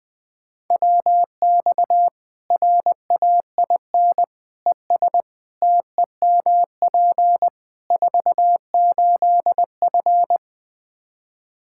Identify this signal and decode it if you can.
Morse code, heard as a keyed tone, about 20 wpm, 700 Hz. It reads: WX RAIN ES TEMP 48F
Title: Practice QSO - Exchange 2 - Info - 20wpm